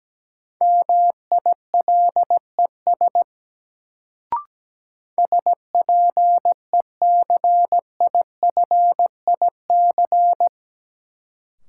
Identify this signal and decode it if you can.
Morse code, heard as a keyed tone, about 17 wpm, 700 Hz. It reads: MILES SPECIFIC
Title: Sentences from Top 1000 Words - Review Part 3 - New Word Drill - 17wpm